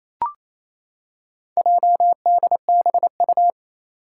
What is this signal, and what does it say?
Morse code, heard as a keyed tone, about 29 wpm, 700 Hz. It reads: JDBU
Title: Four Challenging Letter-Number Combinations 28wpm